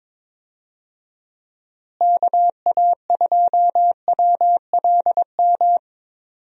Morse code, heard as a keyed tone, about 22 wpm, 700 Hz. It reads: KA2WLM
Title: USA Call Signs - 6 Characters - Repeated 3x - 22wpm